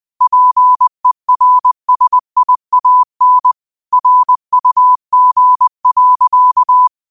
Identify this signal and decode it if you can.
Morse code, heard as a keyed tone, about 20 wpm, 1000 Hz. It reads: PERSIAN RUG.
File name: qotdfiles / 20wpm